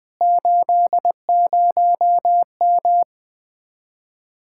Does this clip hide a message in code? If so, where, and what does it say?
Yes: Morse code from 0.2 to 3.0 s, heard as a keyed tone, about 20 wpm, 700 Hz. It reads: 80M